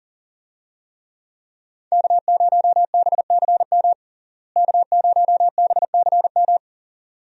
Morse code, heard as a keyed tone, about 40 wpm, 700 Hz. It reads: K0BCM K0BCM
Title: USA Call Signs - 5 Characters - Repeated 2x - 40wpm